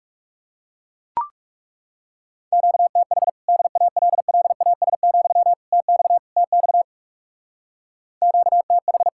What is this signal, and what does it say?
Morse code, heard as a keyed tone, about 36 wpm, 700 Hz. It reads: QTH DALLAS, TX TX QTH
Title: Speed-Racing Common QSO Phrases Encore - 30wpm